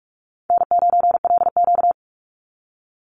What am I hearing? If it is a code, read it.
Morse code, heard as a keyed tone, about 44 wpm, 700 Hz. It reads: N9LQ
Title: Call Signs Repeated 2x - 45wpm